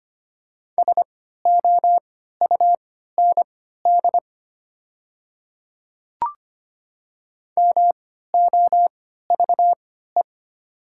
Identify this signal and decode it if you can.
Morse code, heard as a keyed tone, about 25 wpm, spaced as for 14 wpm, 690 Hz. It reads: SOUND MOVE
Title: 3x Character Spacing - Top 500 Words - 25wpm